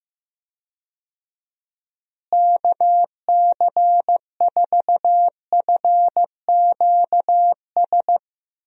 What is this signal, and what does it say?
Morse code, heard as a keyed tone, about 15 wpm, 700 Hz. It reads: KC4FQS
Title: USA Call Signs - 6 Characters - Repeated 3x - 15wpm